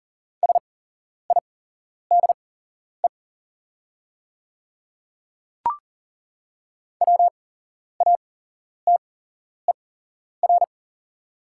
Morse code, heard as a keyed tone, about 39 wpm, spaced as for 11 wpm, 690 Hz. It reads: SIDE WATER
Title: 8x Character Spacing - Top 100 Words - 40wpm